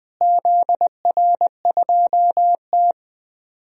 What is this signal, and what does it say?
Morse code, heard as a keyed tone, about 20 wpm, 700 Hz. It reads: ZR2T
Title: DXCC-100 Call Signs - 4 Characters - Repeated 3x - 20wpm